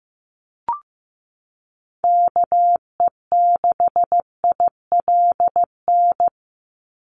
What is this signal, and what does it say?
Morse code, heard as a keyed tone, about 15 wpm, 700 Hz. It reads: KE6ILN
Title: USA Call Signs 15wpm